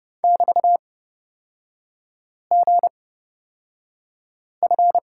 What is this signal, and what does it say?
Morse code, heard as a keyed tone, about 30 wpm, spaced as for 5 wpm, 700 Hz. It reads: <BT>GF
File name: Lesson 36e - BT - Three Characters 30wpm